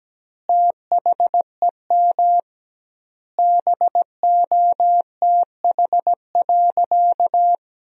Morse code, heard as a keyed tone, about 17 wpm, 700 Hz. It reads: THEM BOTH.
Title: 2x Word-Spacing - Sentences from Top 400 Words - 17wpm